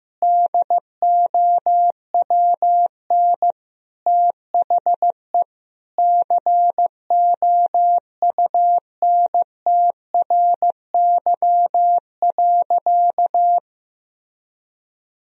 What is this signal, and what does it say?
Morse code, heard as a keyed tone, about 15 wpm, 700 Hz. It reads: DOWN THE COUNTRY.
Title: Sentences from Top 1000 Words - Review Part 2 - 15wpm